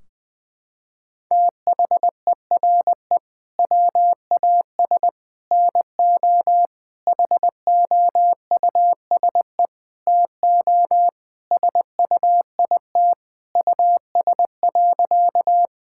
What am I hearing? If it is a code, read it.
Morse code, heard as a keyed tone, about 20 wpm, 700 Hz. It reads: THERE WAS NO HOUSE TO SUIT US.
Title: Sentences from Top 1600 Words 20wpm